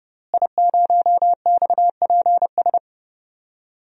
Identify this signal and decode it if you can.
Morse code, heard as a keyed tone, about 30 wpm, 700 Hz. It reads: I0XPS
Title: DXCC-100 Call Signs - 5 Characters - Repeated 2x - 30wpm